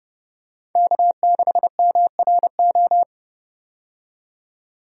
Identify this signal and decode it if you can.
Morse code, heard as a keyed tone, about 32 wpm, 700 Hz. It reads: K6MRO
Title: USA Call Signs - 5 Characters - 30wpm